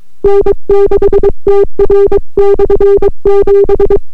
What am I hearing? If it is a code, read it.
Morse code, heard as a keyed tone, about 22 wpm, 400 Hz. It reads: N6TR/7